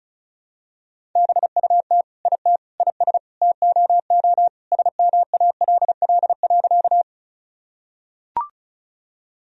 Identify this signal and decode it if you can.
Morse code, heard as a keyed tone, about 34 wpm, 690 Hz. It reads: BUT IT IS TOO SMALL.
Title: Sentences from Top 1000 Words - Review Part 1 - 35wpm